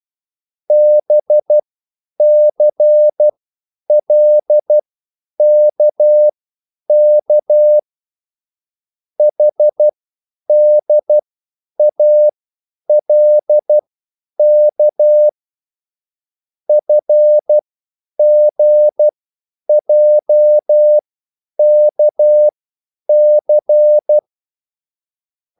Morse code, heard as a keyed tone, about 12 wpm, spaced as for 9 wpm, 600 Hz. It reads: BCLKK HDALK FGJKC
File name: QSL / CW lesson